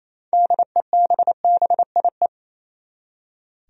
Morse code, heard as a keyed tone, about 28 wpm, 700 Hz. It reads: DEBBIE
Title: USA Common Names - 28wpm